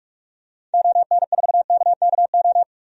Morse code, heard as a keyed tone, about 47 wpm, 700 Hz. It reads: ON4KKO
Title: DXCC-100 Call Signs - 6 Characters - 45wpm